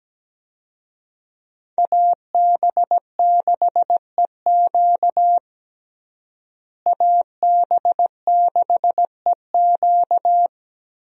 Morse code, heard as a keyed tone, about 17 wpm, 700 Hz. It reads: AB6EQ AB6EQ
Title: USA Call Signs - 5 Characters - Repeated 3x - 17wpm